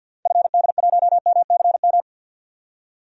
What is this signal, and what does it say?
Morse code, heard as a keyed tone, about 52 wpm, 700 Hz. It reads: FD1MCM
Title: DXCC-100 Call Signs - 6 Characters - Repeated 3x - 50wpm